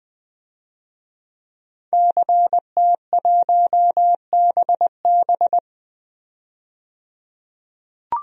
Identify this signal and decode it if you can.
Morse code, heard as a keyed tone, about 20 wpm, 700 Hz. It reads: CT1BB
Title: DXCC-100 Call Signs - 5 Characters - 20wpm